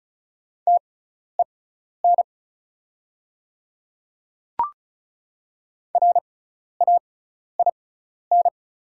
Morse code, heard as a keyed tone, about 36 wpm, spaced as for 12 wpm, 700 Hz. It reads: TEN RAIN
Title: 6x Character Spacing - Top 500 Words - 35wpm